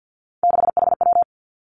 Morse code, heard as a keyed tone, about 50 wpm, 700 Hz. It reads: 6HW
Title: Three Challenging Letter-Number Combinations 50wpm